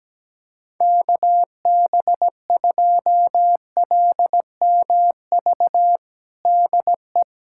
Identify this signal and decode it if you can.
Morse code, heard as a keyed tone, about 17 wpm, 700 Hz. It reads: KB2LMV DE
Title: USA Callsign DE Callsign - 6 Characters - 17wpm